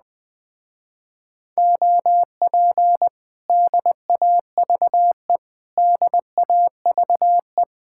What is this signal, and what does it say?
Morse code, heard as a keyed tone, about 20 wpm, 700 Hz. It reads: OP DAVE DAVE
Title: Speed-Racing Common QSO Phrases - 17wpm